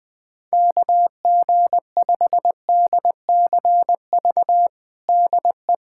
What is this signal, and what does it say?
Morse code, heard as a keyed tone, about 20 wpm, 700 Hz. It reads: KG5DCV DE
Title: USA Callsign DE Callsign - 6 Characters - 20wpm